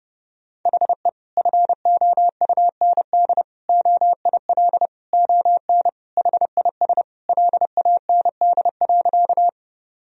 Morse code, heard as a keyed tone, about 30 wpm, 700 Hz. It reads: HE FOUND OIL ON HIS LAND.
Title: Sentences from Top 1000 Words - Review Part 4 - 30wpm